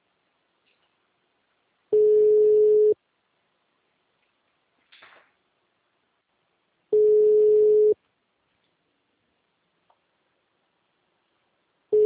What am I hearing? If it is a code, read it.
Ringback tone (Europe, 425 Hz).